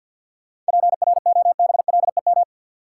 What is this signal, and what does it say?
Morse code, heard as a keyed tone, about 52 wpm, 700 Hz. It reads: PROBLEM
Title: Sentences from Top 1000 Words - New Word Drill - 50wpm